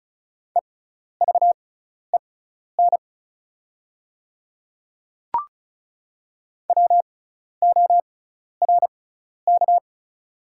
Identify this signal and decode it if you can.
Morse code, heard as a keyed tone, about 34 wpm, spaced as for 12 wpm, 700 Hz. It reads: EVEN WORK
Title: 6x Character Spacing - Top 300 Words - 35wpm